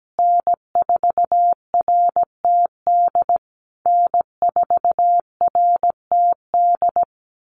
Morse code, heard as a keyed tone, about 17 wpm, 700 Hz. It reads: N4RTD N4RTD
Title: USA Call Signs Repeated 3x 17wpm